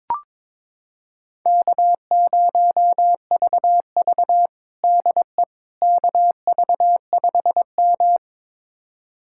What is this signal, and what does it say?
Morse code, heard as a keyed tone, about 22 wpm, 700 Hz. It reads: K0VV DE KV5M
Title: USA Callsign DE Callsign - 4 Characters - 22wpm